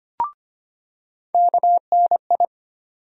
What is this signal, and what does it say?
Morse code, heard as a keyed tone, about 25 wpm, 700 Hz. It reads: KNI
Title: USA Call Sign Postfixes - 3 Letters 25wpm